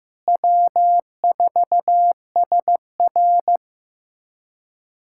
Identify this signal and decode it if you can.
Morse code, heard as a keyed tone, about 15 wpm, 700 Hz. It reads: W4SR
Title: USA Callsign DE Callsign - 4 Characters - 15wpm